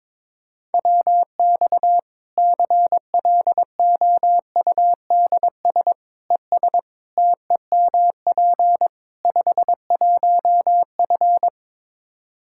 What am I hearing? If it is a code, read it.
Morse code, heard as a keyed tone, about 22 wpm, 700 Hz. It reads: WX CLOUDS ES TEMP 51F
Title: Practice QSO - Exchange 2 - Info - 22wpm